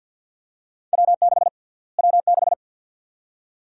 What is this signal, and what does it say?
Morse code, heard as a keyed tone, about 51 wpm, 700 Hz. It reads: W6 W6
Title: USA Call Sign Prefixes - Repeated 2x - 50wpm